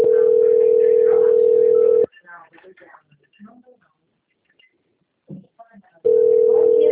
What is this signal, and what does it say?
Ringback tone (North America), 440 + 480 Hz, with speech.
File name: out- - - - - .59202